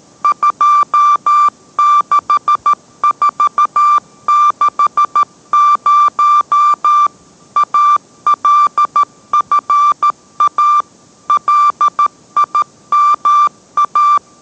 Morse code, heard as a keyed tone, about 13 wpm, 1180 Hz. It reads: 26460 ALFA LIMA